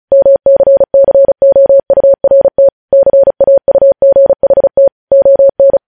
Morse code, heard as a keyed tone, about 36 wpm, 550 Hz. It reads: MCCOURT CAUGHT ON
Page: News headlines in Morse code at 35 wpm